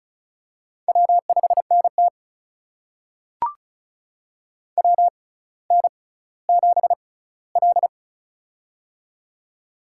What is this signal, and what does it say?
Morse code, heard as a keyed tone, about 36 wpm, spaced as for 18 wpm, 700 Hz. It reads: W5NT WN7L
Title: USA Call Signs - 4 Characters - 6x Character Spacing - 35wpm